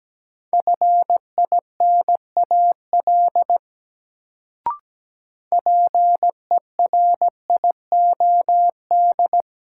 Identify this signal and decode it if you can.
Morse code, heard as a keyed tone, about 17 wpm, 700 Hz. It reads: FINAL PERIOD
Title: Sentences from Top 800 Words - New Word Drill - 17wpm